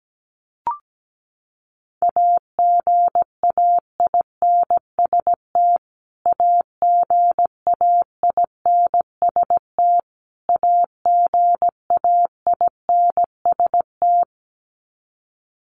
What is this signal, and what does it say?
Morse code, heard as a keyed tone, about 17 wpm, 700 Hz. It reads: AGAINST AGAINST AGAINST
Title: Top 500 Words Repeated 17wpm